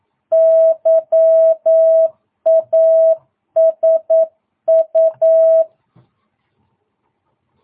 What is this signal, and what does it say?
Morse code, heard as a keyed tone, about 9 wpm, 650 Hz. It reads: YASU
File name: jh8cbh / 2022cw